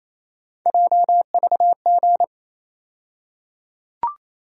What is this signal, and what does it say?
Morse code, heard as a keyed tone, about 28 wpm, 700 Hz. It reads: JVG
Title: USA Call Sign Postfixes - 3 Letters 28wpm